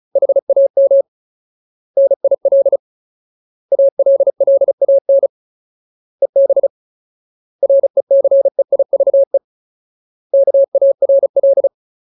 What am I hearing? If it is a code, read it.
Morse code, heard as a keyed tone, about 34 wpm, 550 Hz. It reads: HAM NIL ALLAN EB RECEIVE KARL
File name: Ultimate Word List / 35WPM Ultimate